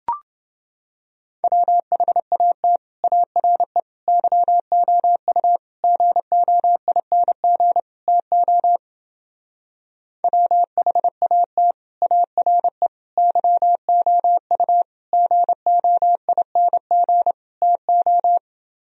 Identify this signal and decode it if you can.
Morse code, heard as a keyed tone, about 29 wpm, 700 Hz. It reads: WHAT ARE YOU GOING TO WHAT ARE YOU GOING TO